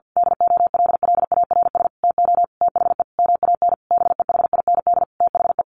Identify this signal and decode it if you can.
Morse code, heard as a keyed tone, about 50 wpm, 700 Hz. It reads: DOLLARS TO THE MAN BEHIND THE
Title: A Sound Of Thunder 50wpm